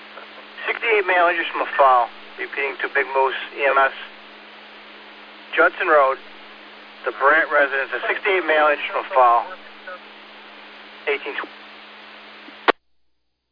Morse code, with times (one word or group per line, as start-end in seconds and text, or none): none